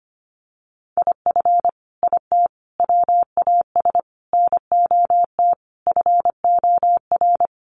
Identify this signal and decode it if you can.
Morse code, heard as a keyed tone, about 25 wpm, 700 Hz. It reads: IF IT WAS NOT FOR
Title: Sets of 5 Words Encore 25wpm